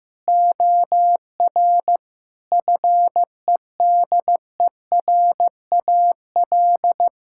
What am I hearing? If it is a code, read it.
Morse code, heard as a keyed tone, about 15 wpm, 700 Hz. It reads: OR FEDERAL
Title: Sentences from Top 1000 Words - Review Part 5 - 15wpm